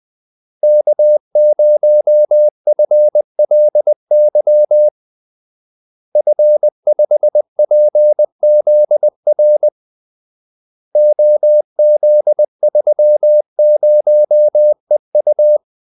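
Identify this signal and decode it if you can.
Morse code, heard as a keyed tone, about 20 wpm, 600 Hz. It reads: K0FLY F5PZR OZ30EU
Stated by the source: Morse call-sign practice at 20 wpm.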